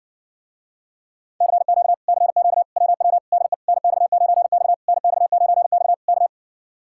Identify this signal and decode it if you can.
Morse code, heard as a keyed tone, about 60 wpm, 700 Hz. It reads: CQ CQ WW DE NX9X NX9X K